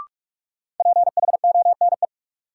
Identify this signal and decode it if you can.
Morse code, heard as a keyed tone, about 46 wpm, 700 Hz. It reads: PHONE
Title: Sentences from Top 1100 Words - New Word Drill Alt - 45wpm